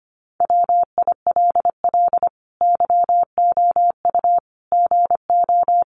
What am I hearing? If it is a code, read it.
Morse code, heard as a keyed tone, about 25 wpm, 700 Hz. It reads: WILL YOU GO
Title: Sentences from Top 1300 Words 25wpm